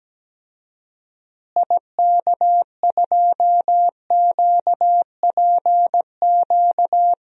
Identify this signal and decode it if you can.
Morse code, heard as a keyed tone, about 17 wpm, 700 Hz. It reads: IK2QPQ
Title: DXCC-100 Call Signs - 6 Characters - Repeated 2x - 17wpm